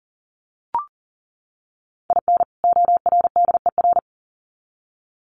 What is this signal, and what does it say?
Morse code, heard as a keyed tone, about 40 wpm, 700 Hz. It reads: IN ORDER
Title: Sets of 2 Words 40wpm